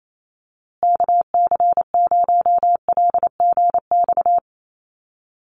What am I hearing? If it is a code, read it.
Morse code, heard as a keyed tone, about 28 wpm, 700 Hz. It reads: KC0LGX
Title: USA Call Signs Repeated 2x 28wpm